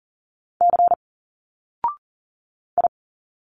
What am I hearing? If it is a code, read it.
Morse code, heard as a keyed tone, about 40 wpm, 700 Hz. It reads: C I
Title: Single Letters 40wpm